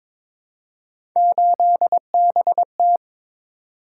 Morse code, heard as a keyed tone, about 21 wpm, 700 Hz. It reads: 8BT
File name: practice / 22wpm / Three Letter-Number Combinations - 22wpm